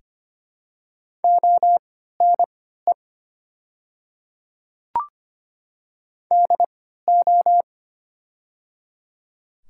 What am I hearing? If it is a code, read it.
Morse code, heard as a keyed tone, about 25 wpm, spaced as for 14 wpm, 700 Hz. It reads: ONE DO
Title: 3x Character Spacing - Top 200 Words - 25wpm